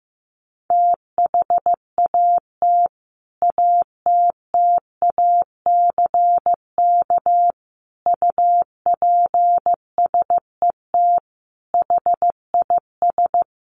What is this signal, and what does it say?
Morse code, heard as a keyed tone, about 15 wpm, 700 Hz. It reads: THAT ATTACK UPSET HIS
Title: Sentences from Top 1500 Words 15wpm